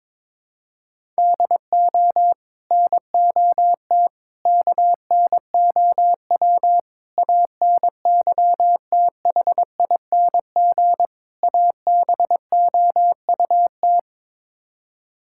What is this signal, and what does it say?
Morse code, heard as a keyed tone, about 22 wpm, 700 Hz. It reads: DO NOT KNOW ANYTHING ABOUT
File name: practice / 22wpm / Build A Phrase - 5-Word Phrases - 22wpm